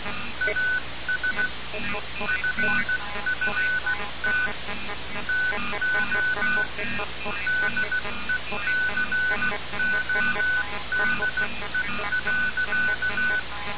Morse code, heard as a keyed tone, about 16 wpm, 1500 Hz, with noise. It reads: AS FFT TO DECODE.
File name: radiocap / REC z